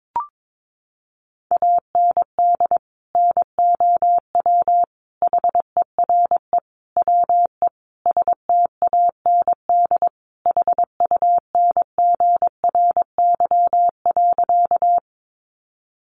Morse code, heard as a keyed tone, about 22 wpm, 700 Hz. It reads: AND NOW HERE WE STAND HUNGRY.
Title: Sentences from Top 1400 Words 22wpm